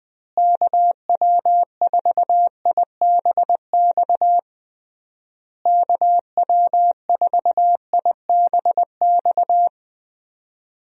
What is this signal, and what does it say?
Morse code, heard as a keyed tone, about 19 wpm, 700 Hz. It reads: KW4IBX KW4IBX
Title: USA Call Signs - 6 Characters - Repeated 3x - 20wpm